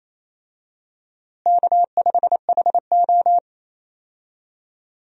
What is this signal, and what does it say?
Morse code, heard as a keyed tone, about 28 wpm, 700 Hz. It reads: K5HO
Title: USA Call Signs - 4 Characters - Repeated 3x - 28wpm